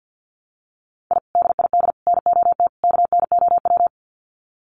Morse code, heard as a keyed tone, about 49 wpm, 700 Hz. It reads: I DID NOT KNOW
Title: Sets of 4 Words 50wpm